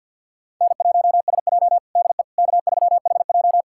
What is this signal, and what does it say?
Morse code, heard as a keyed tone, about 53 wpm, 700 Hz. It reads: N1SJ DE K3HJ
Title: USA Callsign DE Callsign - 4 Characters - 50wpm